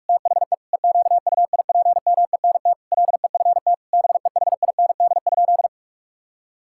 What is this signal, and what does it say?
Morse code, heard as a keyed tone, about 41 wpm, 700 Hz. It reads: THE EQUIPMENT LEFT BEHIND?
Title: Sentences from Top 1000 Words - Review Part 5 - 45wpm